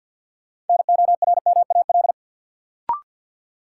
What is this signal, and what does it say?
Morse code, heard as a keyed tone, about 49 wpm, 700 Hz. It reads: NORMAL
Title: Sentences from Top 1200 Words - New Word Drill Alt - 50wpm